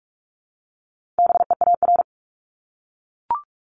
Morse code, heard as a keyed tone, about 44 wpm, 700 Hz. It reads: BEAR
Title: ING Suffix - Top 100 Words with Base Words 45wpm